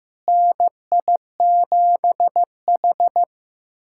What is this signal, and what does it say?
Morse code, heard as a keyed tone, about 15 wpm, 700 Hz. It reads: NI7H
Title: SOTA QSO Practice 15wpm